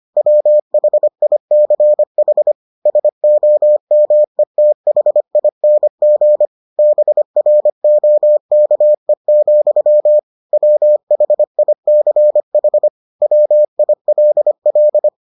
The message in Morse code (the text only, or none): WHICH SOMETHING BROKE, WHICH WILL